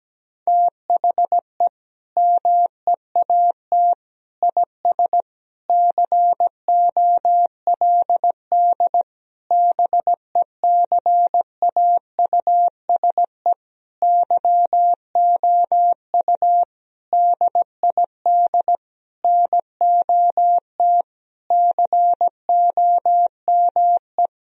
Morse code, heard as a keyed tone, about 17 wpm, 700 Hz. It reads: THE MEAT IS COLD BECAUSE YOU DID NOT COME
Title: Sentences from Top 1000 Words - Review Part 3 - 17wpm